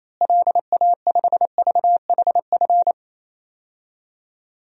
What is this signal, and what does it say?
Morse code, heard as a keyed tone, about 28 wpm, 700 Hz. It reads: LA5VHF
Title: DXCC-100 Call Signs - 6 Characters - Repeated 3x - 28wpm